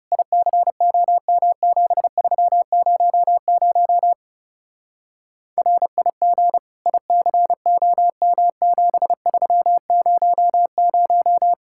Morse code, heard as a keyed tone, about 37 wpm, 700 Hz. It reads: ICOM7300 RIG ICOM7300
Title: Speed-Racing Common QSO Phrases Encore - 30wpm